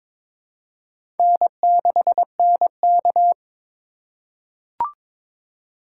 Morse code, heard as a keyed tone, about 21 wpm, 700 Hz. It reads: N6NK
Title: USA Call Signs - 4 Characters - 3x Character Spacing - 22wpm